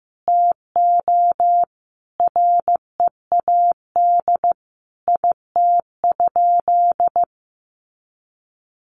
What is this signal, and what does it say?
Morse code, heard as a keyed tone, about 15 wpm, 700 Hz. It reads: TO READ IT?
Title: Sentences from Top 1200 Words 15wpm